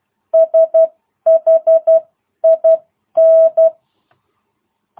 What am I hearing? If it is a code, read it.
Morse code, heard as a keyed tone, about 12 wpm, spaced as for 10 wpm, 660 Hz. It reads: SHIN